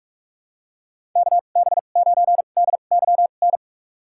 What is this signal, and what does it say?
Morse code, heard as a keyed tone, about 46 wpm, spaced as for 32 wpm, 700 Hz. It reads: KB9DYN
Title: USA Call Signs - 2x Character Spacing - 45wpm